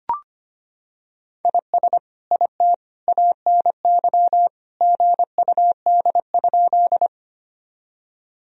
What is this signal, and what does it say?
Morse code, heard as a keyed tone, about 25 wpm, 700 Hz. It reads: IS IT ANY GUD?
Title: Sentences from Top 200 Words and Abbreviations 25wpm